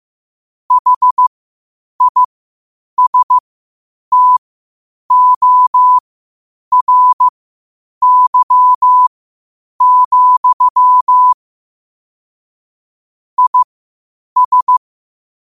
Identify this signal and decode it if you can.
Morse code, heard as a keyed tone, about 15 wpm, spaced as for 8 wpm, 1000 Hz. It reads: HISTORY, IS
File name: qotdfiles / 8wpm